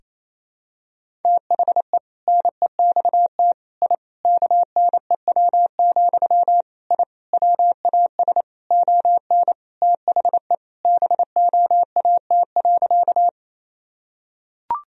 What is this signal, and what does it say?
Morse code, heard as a keyed tone, about 28 wpm, 700 Hz. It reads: THE NEXT I KNEW, I WAS ON THE BOAT.
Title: Sentences from Top 1000 Words - Review Part 3 - 28wpm